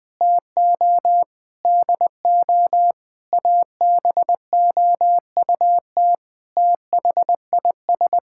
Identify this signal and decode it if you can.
Morse code, heard as a keyed tone, about 20 wpm, 700 Hz. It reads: TO DO ABOUT THIS
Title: Sentences with 8-Word Phrases - Spoken Beginning - 20wpm